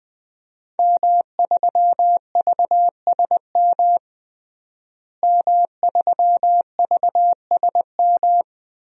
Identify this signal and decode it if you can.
Morse code, heard as a keyed tone, about 21 wpm, 700 Hz. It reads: M3VSM M3VSM
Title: DXCC-100 Call Signs - 5 Characters - Repeated 3x - 20wpm